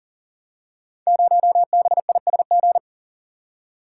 Morse code, heard as a keyed tone, about 41 wpm, 700 Hz. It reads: 0BISG
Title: Code Groups - 5 Letter:Numbers - 40wpm